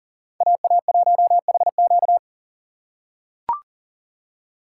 Morse code, heard as a keyed tone, about 40 wpm, 700 Hz. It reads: AA1HQ
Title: USA Call Signs - 2x Character Spacing - 40wpm